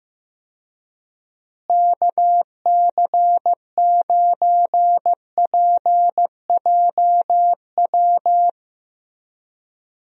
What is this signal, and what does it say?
Morse code, heard as a keyed tone, about 15 wpm, 700 Hz. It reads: KC9PJW
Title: USA Call Signs - 6 Characters - Repeated 3x - 15wpm